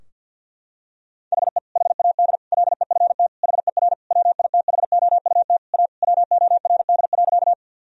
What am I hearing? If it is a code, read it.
Morse code, heard as a keyed tone, about 52 wpm, 700 Hz. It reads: HE HAD LEFT HER WITHOUT A WORD.
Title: Sentences from Top 1000 Words - Review Part 3 - 50wpm